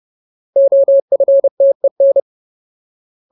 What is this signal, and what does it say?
Morse code, heard as a keyed tone, about 30 wpm, 550 Hz. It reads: OFTEN
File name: Ultimate Word List / 30WPM Ultimate